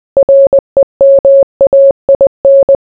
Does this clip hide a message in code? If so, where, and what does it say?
Yes: Morse code from 0.2 to 2.8 s, heard as a keyed tone, about 20 wpm, 550 Hz. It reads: REMAIN